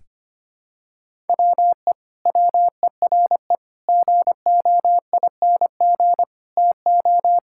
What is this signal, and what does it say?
Morse code, heard as a keyed tone, about 25 wpm, 700 Hz. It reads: WE WERE GOING TO
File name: practice / 25wpm / Sentences with 4-Word Phrases - 25wpm